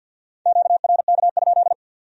Morse code, heard as a keyed tone, about 62 wpm, 700 Hz. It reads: QRK?
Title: Speed-Racing Q-Codes - 50wpm